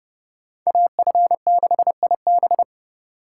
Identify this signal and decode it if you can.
Morse code, heard as a keyed tone, about 29 wpm, 700 Hz. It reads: AF6IB